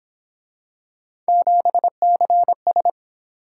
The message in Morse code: 7CS

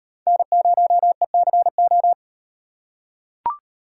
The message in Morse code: N0ECO

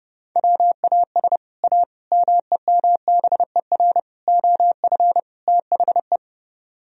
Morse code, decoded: WAS A MEMBER OF THE

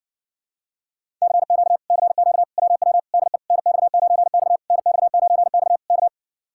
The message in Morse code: CQ CQ WW DE NX9X NX9X K

